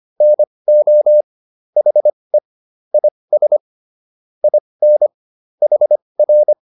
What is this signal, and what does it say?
Morse code, heard as a keyed tone, about 25 wpm, spaced as for 20 wpm, 600 Hz. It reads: NO HE IS IN HR